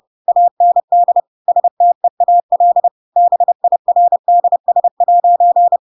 Morse code, heard as a keyed tone, about 30 wpm, 700 Hz. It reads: AND STEAL BIRDS'
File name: Pinocchio / Pinocchio ch4 30 30 ik1hge